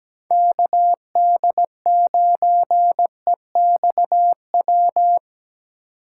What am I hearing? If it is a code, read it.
Morse code, heard as a keyed tone, about 17 wpm, 700 Hz. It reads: KD9EXW
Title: USA Call Signs - 6 Characters - 3x Character Spacing - 17wpm